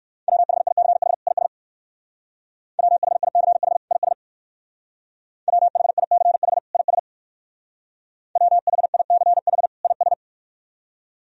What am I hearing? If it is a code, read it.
Morse code, heard as a keyed tone, about 59 wpm, 700 Hz. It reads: WHICH IS WHICH IS WHICH IS WHICH IS